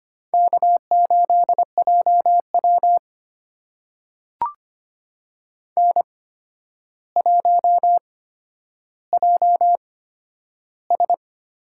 Morse code, heard as a keyed tone, about 25 wpm, spaced as for 11 wpm, 700 Hz. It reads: K8JW N1JS
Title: USA Call Signs - 4 Characters - 8x Character Spacing - 25wpm